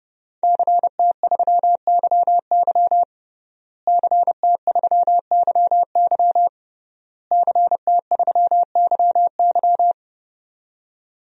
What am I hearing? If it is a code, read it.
Morse code, heard as a keyed tone, about 32 wpm, 700 Hz. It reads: CT3YY CT3YY CT3YY
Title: DXCC-100 Call Signs - 5 Characters - Repeated 3x - 30wpm